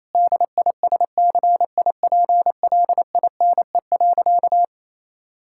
Morse code, heard as a keyed tone, about 28 wpm, 700 Hz. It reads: DISCIPLINE.
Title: Sentences from Top 1000 Words - Review Part 6 - 28wpm